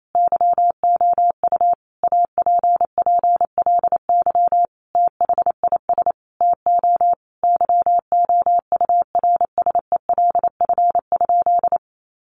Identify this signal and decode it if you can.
Morse code, heard as a keyed tone, about 28 wpm, 700 Hz. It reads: YOU APPLY THIS TO YOURSELF?
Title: Sentences from Top 1200 Words 28wpm